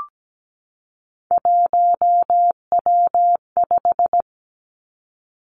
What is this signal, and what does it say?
Morse code, heard as a keyed tone, about 17 wpm, 700 Hz. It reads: 1W5
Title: Three Challenging Letter-Number Combinations 17wpm